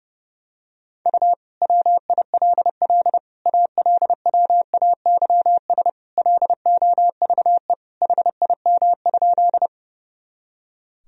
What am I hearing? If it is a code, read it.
Morse code, heard as a keyed tone, about 32 wpm, 700 Hz. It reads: U WILL ALWAYS LOVE HIM?
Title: Sentences from Top 400 Words and Abbreviations 30wpm